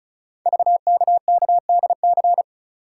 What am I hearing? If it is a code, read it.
Morse code, heard as a keyed tone, about 36 wpm, 690 Hz. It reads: VKKDC
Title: Code Groups - 5 Letters - 35wpm